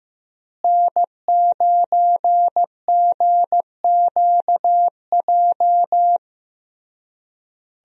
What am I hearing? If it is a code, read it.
Morse code, heard as a keyed tone, about 15 wpm, 700 Hz. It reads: N9GQJ
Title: USA Call Signs - 5 Characters - Repeated 3x - 15wpm